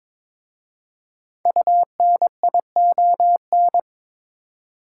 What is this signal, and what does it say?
Morse code, heard as a keyed tone, about 22 wpm, 700 Hz. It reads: UNION
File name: 5 Letter Words - First 3 Letters Given - 22wpm